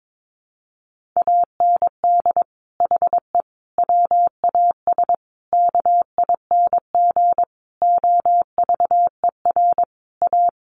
Morse code, heard as a keyed tone, about 22 wpm, 700 Hz. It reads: AND HE WAS KING OVER A
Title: Sentences from Top 1400 Words 22wpm